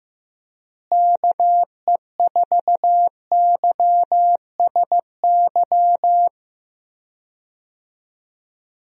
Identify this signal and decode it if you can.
Morse code, heard as a keyed tone, about 16 wpm, 700 Hz. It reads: KE4YSY
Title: USA Call Signs - 6 Characters - Repeated 2x - 15wpm